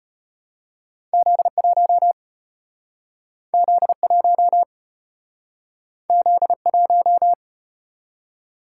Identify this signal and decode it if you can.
Morse code, heard as a keyed tone, about 34 wpm, 700 Hz. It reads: Z1 Z1 Z1